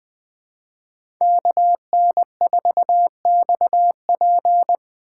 Morse code, heard as a keyed tone, about 20 wpm, 700 Hz. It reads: KN4XP
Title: 2023 POTA Hunter Call Signs - 20wpm